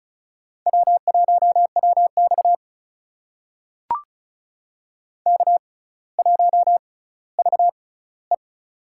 Morse code, heard as a keyed tone, about 36 wpm, spaced as for 18 wpm, 700 Hz. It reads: W1WX K1VE
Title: USA Call Signs - 6x Character Spacing - 35wpm